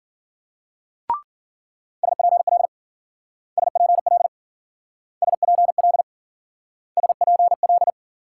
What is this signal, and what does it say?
Morse code, heard as a keyed tone, about 50 wpm, 700 Hz. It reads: SPL SPL SPL SPL